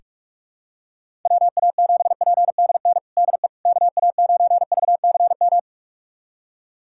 Morse code, heard as a keyed tone, about 46 wpm, 700 Hz. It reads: WA7PDN DE KA9VCM
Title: USA Callsign DE Callsign 45wpm